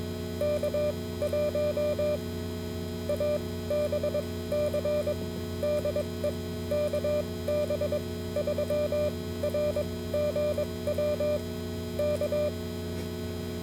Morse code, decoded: K1ABC DE KB3RGW K